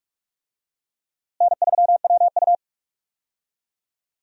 Morse code, heard as a keyed tone, about 46 wpm, 700 Hz. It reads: N3WU
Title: USA Call Signs - 4 Characters - Repeated 2x - 45wpm